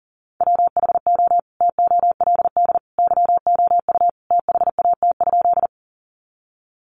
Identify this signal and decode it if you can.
Morse code, heard as a keyed tone, about 40 wpm, 700 Hz. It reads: WHO TOLD YOU THAT?
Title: Sentences from Top 400 Words 40wpm